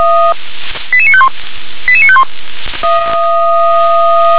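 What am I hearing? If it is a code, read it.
Five-tone selective call (ZVEI), heard as quick runs of five tones, tones of about 70 ms. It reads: 88951 88951
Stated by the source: ZVEI five-tone selective call received on an SDR.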